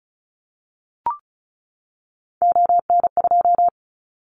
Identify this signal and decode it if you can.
Morse code, heard as a keyed tone, about 34 wpm, 700 Hz. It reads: ON2
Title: Callsign Prefixes for Top 50 DXCC - 35wpm